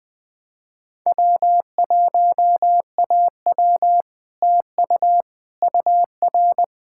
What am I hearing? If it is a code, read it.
Morse code, heard as a keyed tone, about 20 wpm, 700 Hz. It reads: W1AW TU UR
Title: Speed-Racing Common QSO Phrases - 17wpm